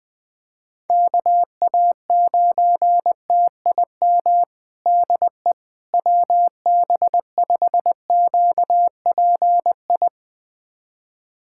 Morse code, heard as a keyed tone, about 21 wpm, 700 Hz. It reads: KA9TIM DE WB5QPI
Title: USA Callsign DE Callsign - 6 Characters - 20wpm